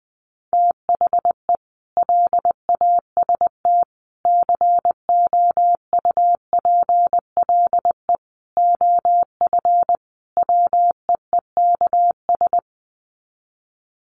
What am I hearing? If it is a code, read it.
Morse code, heard as a keyed tone, about 20 wpm, 700 Hz. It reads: THE LAST COUPLE OF WEEKS
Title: Sets of 5 Words Encore2 20wpm